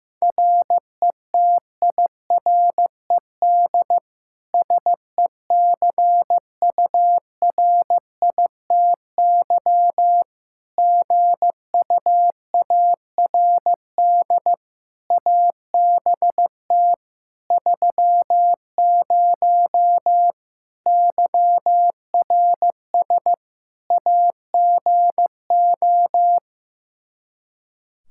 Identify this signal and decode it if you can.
Morse code, heard as a keyed tone, about 16 wpm, 700 Hz. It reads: RETIRED SECURITY GUARD ABT 30 YRS AGO
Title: K1USN SST QSO Practice 15wpm